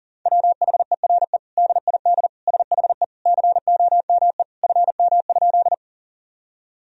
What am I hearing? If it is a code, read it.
Morse code, heard as a keyed tone, about 37 wpm, 700 Hz. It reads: WHERE DID SHE COME FM?
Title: Sentences from Top 200 Words and Abbreviations 40wpm